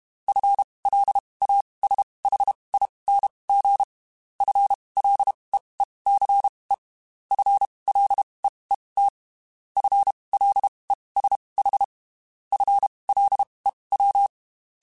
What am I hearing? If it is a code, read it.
Morse code, heard as a keyed tone, about 32 wpm, spaced as for 23 wpm, 800 Hz. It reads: FLASHING FLEECE FLEET FLESH FLEW
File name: Spaced / FL Runs sp mp3s / FL Runs sp @30